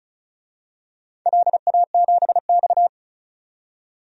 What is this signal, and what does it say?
Morse code, heard as a keyed tone, about 34 wpm, 700 Hz. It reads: LA7X
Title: DXCC-100 Call Signs - 4 Characters - 35wpm